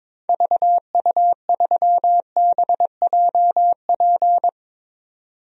Morse code, heard as a keyed tone, about 22 wpm, 700 Hz. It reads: VU3BJP
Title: DXCC-100 Call Signs - 6 Characters - 22wpm